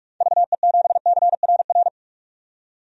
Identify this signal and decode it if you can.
Morse code, heard as a keyed tone, about 46 wpm, 700 Hz. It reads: VE7CRR
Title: DXCC-100 Call Signs - 6 Characters - 45wpm